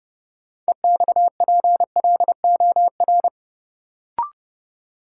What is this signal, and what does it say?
Morse code, heard as a keyed tone, about 30 wpm, 700 Hz. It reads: EXPLOR
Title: Sentences from Top 1100 Words - New Word Drill Alt - 30wpm